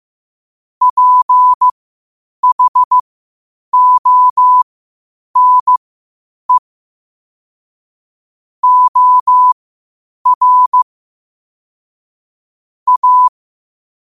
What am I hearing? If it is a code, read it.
Morse code, heard as a keyed tone, about 15 wpm, spaced as for 8 wpm, 1000 Hz. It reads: PHONE OR A
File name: qotdfiles / 8wpm